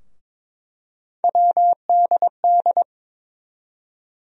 Morse code, heard as a keyed tone, about 22 wpm, 700 Hz. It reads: WDD
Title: USA Call Sign Postfixes - 3 Letters 22wpm